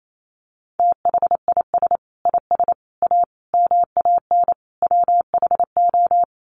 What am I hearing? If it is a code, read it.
Morse code, heard as a keyed tone, about 28 wpm, 700 Hz. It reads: THIS IS A MAN WHO
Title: Sets of 5 Words Encore2 28wpm